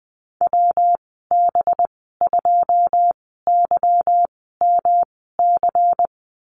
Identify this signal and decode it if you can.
Morse code, heard as a keyed tone, about 20 wpm, spaced as for 15 wpm, 700 Hz. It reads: WB2YMC
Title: 207 Lesson 33g - Letter j - Call Signs 20wpm